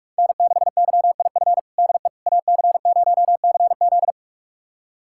Morse code, heard as a keyed tone, about 43 wpm, 700 Hz. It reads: N6YIF DE AC0CZ
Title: USA Callsign DE Callsign - 5 Characters - 45wpm